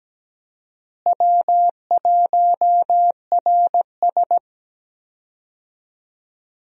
Morse code, heard as a keyed tone, about 17 wpm, 700 Hz. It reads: W1RS
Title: USA Call Signs - 4 Characters - Repeated 3x - 17wpm